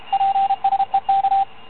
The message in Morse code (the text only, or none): PSEK